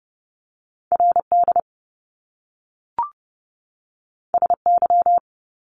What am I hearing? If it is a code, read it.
Morse code, heard as a keyed tone, about 30 wpm, 700 Hz. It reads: RD SY
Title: Top Two Letter Combinations in Words 30wpm